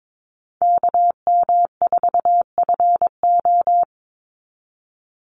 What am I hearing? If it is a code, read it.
Morse code, heard as a keyed tone, about 22 wpm, 700 Hz. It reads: KM4FO
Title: Call Signs Repeated 2x - 22wpm